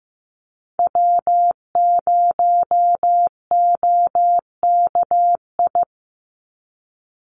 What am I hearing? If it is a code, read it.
Morse code, heard as a keyed tone, about 14 wpm, 700 Hz. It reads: W0OKI
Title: USA Call Signs 15wpm